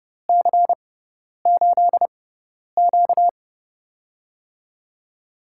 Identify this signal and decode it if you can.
Morse code, heard as a keyed tone, about 30 wpm, spaced as for 10 wpm, 700 Hz. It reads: C8Q
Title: USA Call Signs - 4 Characters - 6x Character Spacing - 30wpm